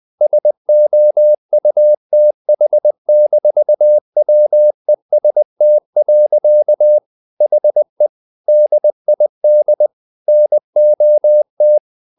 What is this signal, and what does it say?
Morse code, heard as a keyed tone, about 20 wpm, 600 Hz. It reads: SOUTH-WEST. HE DID NOT
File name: the old man and the sea part 2 20wpm